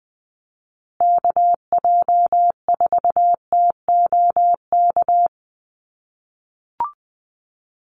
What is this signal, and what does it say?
Morse code, heard as a keyed tone, about 20 wpm, 700 Hz. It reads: KJ4TOK
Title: USA Call Signs Repeated 3x 20wpm